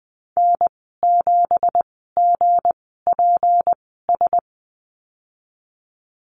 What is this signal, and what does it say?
Morse code, heard as a keyed tone, about 20 wpm, spaced as for 15 wpm, 700 Hz. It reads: N7GPS
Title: 167 Lesson 27g - Number 7 - Call Signs 20wpm